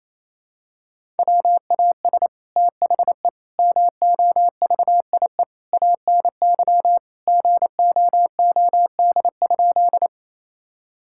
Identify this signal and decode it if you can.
Morse code, heard as a keyed tone, about 28 wpm, 700 Hz. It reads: WAS THE MOVIE ANY GOOD?
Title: Sentences from Top 1000 Words - Review Part 5 - 28wpm